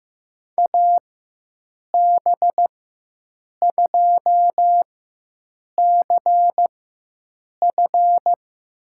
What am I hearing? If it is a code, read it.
Morse code, heard as a keyed tone, about 15 wpm, spaced as for 7 wpm, 700 Hz. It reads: AB2CF
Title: USA Call Signs - 5 Characters - 4x Character Spacing - 15wpm